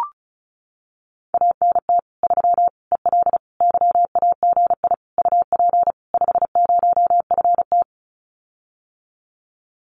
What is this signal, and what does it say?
Morse code, heard as a keyed tone, about 34 wpm, 700 Hz. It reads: ANT 3 EL YAGI UP 50FT
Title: Common QSO Phrases Encore 35wpm